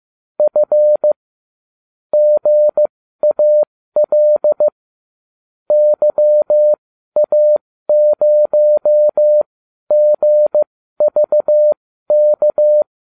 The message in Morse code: F GAL YA0GVK

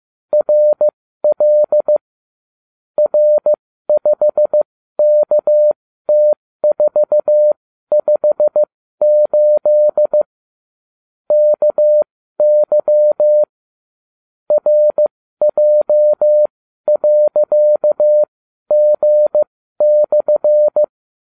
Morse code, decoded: RL R5KT458 KY RJ.G/

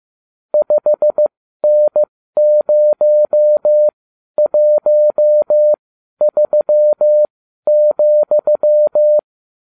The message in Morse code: 5N013,

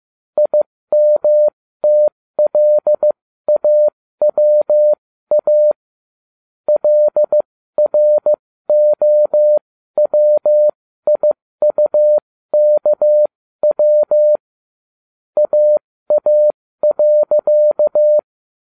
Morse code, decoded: IMTLAWA LROWIUKW AA.

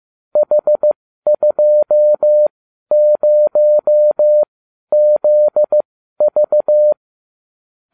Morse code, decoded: H20ZV

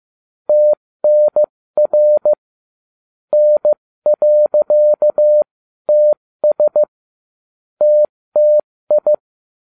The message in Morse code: TNR N.TS TTI

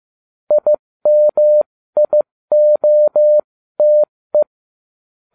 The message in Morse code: IMIOTE

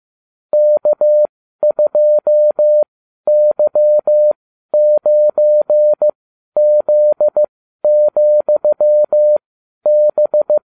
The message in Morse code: K2Y9Z,B

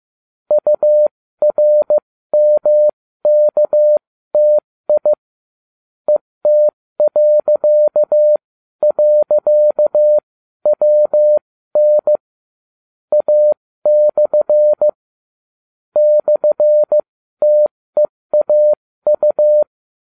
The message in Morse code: URMKTI ET..WN A/ /TEAU

